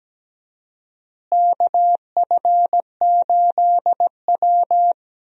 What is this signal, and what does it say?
Morse code, heard as a keyed tone, about 17 wpm, 700 Hz. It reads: KF8W
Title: USA Call Signs - 4 Characters - Repeated 3x - 17wpm